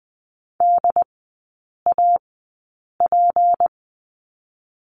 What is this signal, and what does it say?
Morse code, heard as a keyed tone, about 20 wpm, spaced as for 8 wpm, 690 Hz. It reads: DAP
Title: 191 Lesson 31e - Letter v - Three Characters 20wpm